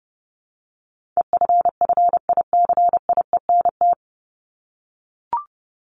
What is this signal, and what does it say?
Morse code, heard as a keyed tone, about 29 wpm, 700 Hz. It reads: EFFICIENT
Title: Top 1251-1500 Words 30wpm